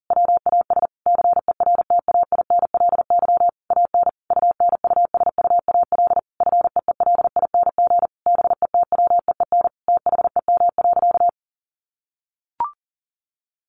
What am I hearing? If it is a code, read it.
Morse code, heard as a keyed tone, about 40 wpm, 700 Hz. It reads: WAS CERTAINLY AN UNUSUAL FEELING BETWEEN THEM.
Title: Sentences from Top 1400 Words 40wpm